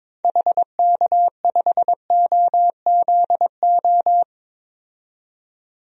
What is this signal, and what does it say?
Morse code, heard as a keyed tone, about 22 wpm, 700 Hz. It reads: HK5OZO
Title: DXCC-100 Call Signs - 6 Characters - 22wpm